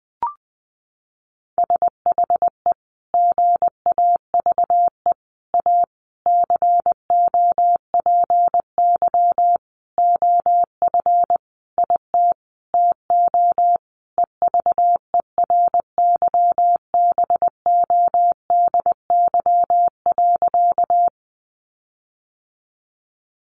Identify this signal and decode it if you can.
Morse code, heard as a keyed tone, about 20 wpm, 700 Hz. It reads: SHE GAVE A COPY OF IT TO EVERYBODY.
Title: Sentences from Top 1400 Words 20wpm